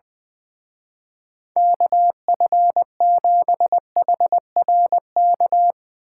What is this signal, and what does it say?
Morse code, heard as a keyed tone, about 22 wpm, 700 Hz. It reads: KF7HRK
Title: Speed Racing - USA 2x3 Call Signs 20wpm